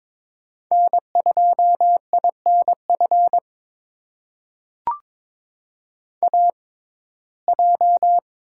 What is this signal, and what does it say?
Morse code, heard as a keyed tone, about 22 wpm, spaced as for 16 wpm, 700 Hz. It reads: N2INF AJ
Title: USA Call Signs - 5 Characters - 6x Character Spacing - 22wpm